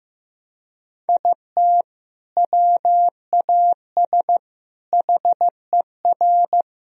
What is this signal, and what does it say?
Morse code, heard as a keyed tone, about 15 wpm, 700 Hz. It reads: IT WAS HER
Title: Sentences from Top 1000 Words - Review Part 2 - 15wpm